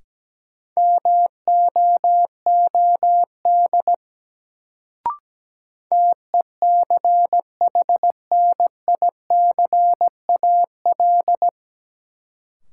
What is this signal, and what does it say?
Morse code, heard as a keyed tone, about 17 wpm, 700 Hz. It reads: MOOD TECHNICAL
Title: Sentences from Top 1100 Words - New Word Drill Alt - 17wpm